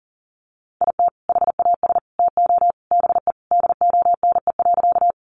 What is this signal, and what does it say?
Morse code, heard as a keyed tone, about 40 wpm, 700 Hz. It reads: IT HAS TO BE DONE.
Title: Sentences from Top 500 Words 40wpm